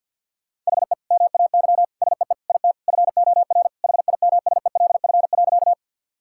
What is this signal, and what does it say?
Morse code, heard as a keyed tone, about 50 wpm, 700 Hz. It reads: HE MAY SEE IT FOR HIMSELF.
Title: Sentences from Top 1600 Words 50wpm